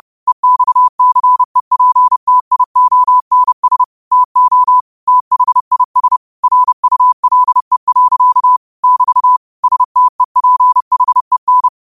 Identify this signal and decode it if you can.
Morse code, heard as a keyed tone, about 30 wpm, 1000 Hz. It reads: EXCEPTIONS TO THIS RULE. <BT> STEPHEN
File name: qotdfiles / 30wpm